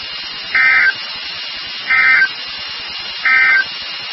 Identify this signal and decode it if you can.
Emergency Alert System (SAME) tones, heard as harsh buzzing bursts. It is heard from 0.4 to 3.7 s.
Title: NOAA Weather Radio WXK36